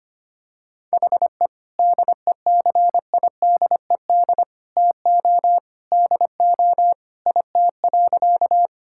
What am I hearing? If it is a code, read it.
Morse code, heard as a keyed tone, about 25 wpm, 700 Hz. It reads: HE DECIDED TO DO IT.
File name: practice / 25wpm / Sentences with 3-Word Phrases - 25wpm